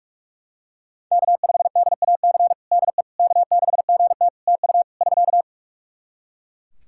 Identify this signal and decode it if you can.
Morse code, heard as a keyed tone, about 41 wpm, 700 Hz. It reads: K5DAC DE K6GT TU <SK>
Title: Practice QSO - Exchange 4 - Ending - 45wpm